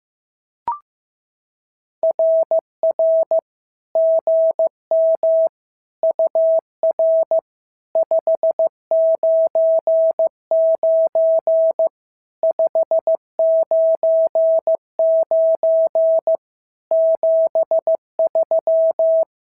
Morse code, heard as a keyed tone, about 15 wpm, 650 Hz. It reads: RR GM UR 599 599 73
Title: SOTA QSO Practice 15wpm